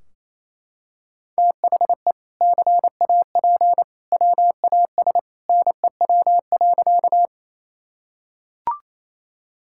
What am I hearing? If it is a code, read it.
Morse code, heard as a keyed tone, about 28 wpm, 700 Hz. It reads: THE CAP WAS NEW.
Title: Sentences from Top 1600 Words 28wpm